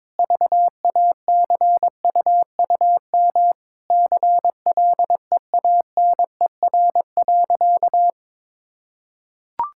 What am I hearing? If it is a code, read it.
Morse code, heard as a keyed tone, about 21 wpm, 700 Hz. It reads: VACUUM CLEANER.
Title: Pangram Challenge v2 - 22wpm